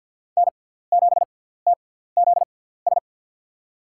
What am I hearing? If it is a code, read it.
Morse code, heard as a keyed tone, about 47 wpm, spaced as for 17 wpm, 700 Hz. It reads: N7TZS
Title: USA Call Signs - 5 Characters - 6x Character Spacing - 50wpm